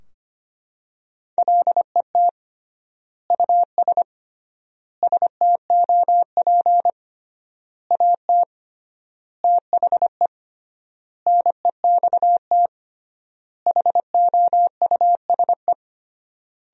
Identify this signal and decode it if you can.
Morse code, heard as a keyed tone, about 25 wpm, 700 Hz. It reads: LET US STOP AT THE NEXT HOUSE
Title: 3x Word-Spacing - Sentences from Top 400 Words - 25wpm